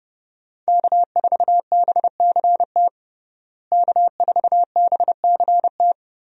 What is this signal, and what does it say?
Morse code, heard as a keyed tone, about 32 wpm, 700 Hz. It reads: K4BCT K4BCT
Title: USA Call Signs - 5 Characters - Repeated 3x - 30wpm